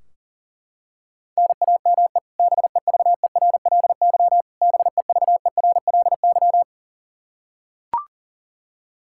Morse code, heard as a keyed tone, about 40 wpm, 700 Hz. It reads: NAME BEVERLY BEVERLY
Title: Practice QSO - Exchange 1 - Beginning - 40wpm